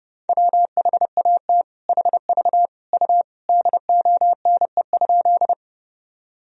Morse code, heard as a keyed tone, about 32 wpm, 700 Hz. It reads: WHAT HV U DONE?
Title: Sentences from Top 500 Words and Abbreviations 30wpm